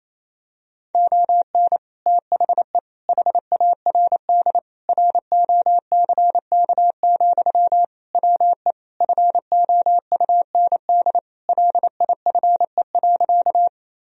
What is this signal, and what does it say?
Morse code, heard as a keyed tone, about 28 wpm, 700 Hz. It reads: ON THE HARD ROCK, WE FOUND LIFE.
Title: Sentences from Top 1000 Words - Review Part 2 - 28wpm